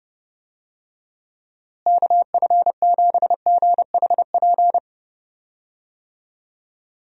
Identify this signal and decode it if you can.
Morse code, heard as a keyed tone, about 30 wpm, 700 Hz. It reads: KF7GHP
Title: USA Call Signs - 6 Characters - Repeated 2x - 30wpm